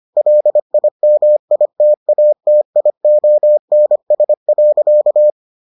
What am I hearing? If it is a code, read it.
Morse code, heard as a keyed tone, about 25 wpm, 600 Hz. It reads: LIMITATIONS.